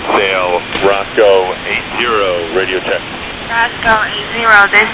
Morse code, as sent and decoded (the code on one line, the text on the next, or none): none